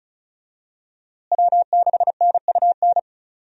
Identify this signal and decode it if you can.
Morse code, heard as a keyed tone, about 34 wpm, 700 Hz. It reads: W6NUN
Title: USA Call Signs - 5 Characters - Repeated 2x - 35wpm